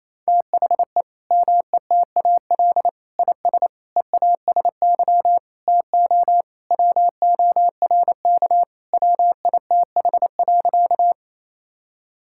Morse code, heard as a keyed tone, about 28 wpm, 700 Hz. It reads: THE METAL IS EASY TO WORK WITH.
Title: Sentences from Top 1000 Words - Review Part 6 - 28wpm